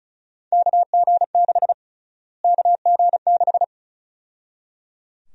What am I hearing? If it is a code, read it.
Morse code, heard as a keyed tone, about 36 wpm, 700 Hz. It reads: KG6 KG6
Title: USA Call Sign Prefixes - Repeated 2x - 35wpm